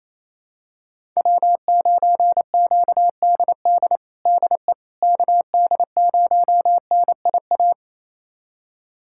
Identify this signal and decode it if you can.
Morse code, heard as a keyed tone, about 28 wpm, 700 Hz. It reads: W9QDD DE KD0NIA
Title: USA Callsign DE Callsign 28wpm